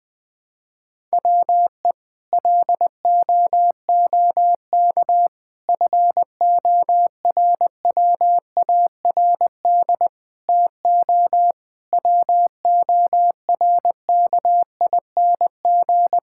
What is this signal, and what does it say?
Morse code, heard as a keyed tone, about 20 wpm, 700 Hz. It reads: WE LOOK FORWARD TO WORKING
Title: Sentences with 6-Word Phrases - 20wpm